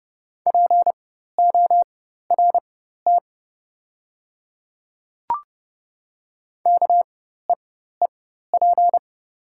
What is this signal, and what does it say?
Morse code, heard as a keyed tone, about 30 wpm, spaced as for 14 wpm, 700 Hz. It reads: PORT KEEP
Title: 4x Character Spacing - Top 500 Words - 30wpm